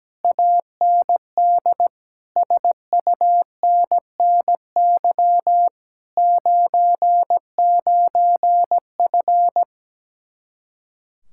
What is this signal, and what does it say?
Morse code, heard as a keyed tone, about 17 wpm, 700 Hz. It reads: AND SUNNY 99F
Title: Speed-Racing Common QSO Phrases - 17wpm